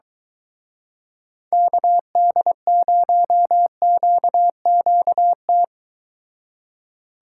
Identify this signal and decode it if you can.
Morse code, heard as a keyed tone, about 24 wpm, 700 Hz. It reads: KD0QQT